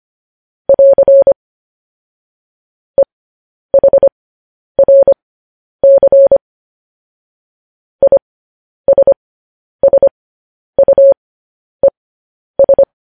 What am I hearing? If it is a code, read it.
Morse code, heard as a keyed tone, about 25 wpm, spaced as for 10 wpm, 550 Hz. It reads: <AR> EHRC ISSUES
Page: news headlines in morse code at 10 wpm